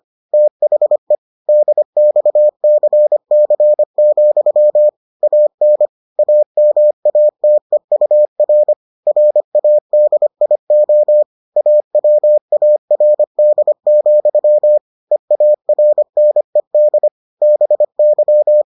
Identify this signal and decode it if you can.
Morse code, heard as a keyed tone, about 25 wpm, 600 Hz. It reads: THE DXCC, AN AMATEUR RADIO AWARD, EARNED BY